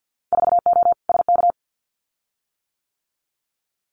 Morse code, heard as a keyed tone, about 50 wpm, 700 Hz. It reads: 4O SK
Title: Winter Field Day Exchanges Encore 50wpm